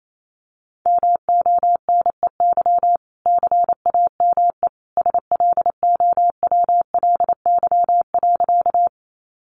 Morse code, heard as a keyed tone, about 28 wpm, 700 Hz. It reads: MONEY CAME SLOWLY.
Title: Sentences from Top 1100 Words 28wpm